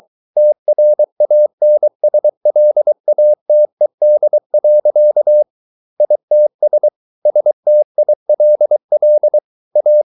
Morse code, heard as a keyed tone, about 23 wpm, 600 Hz. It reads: TRANSLATED. ITS STILL A